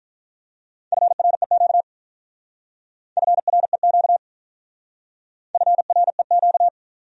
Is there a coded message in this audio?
Yes: Morse code from 0.9 to 6.7 s, heard as a keyed tone, about 45 wpm, 700 Hz. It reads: FREQ FREQ FREQ